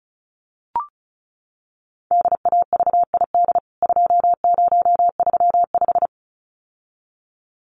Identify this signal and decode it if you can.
Morse code, heard as a keyed tone, about 34 wpm, 700 Hz. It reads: DAVID 2035
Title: CWT Exchange 35wpm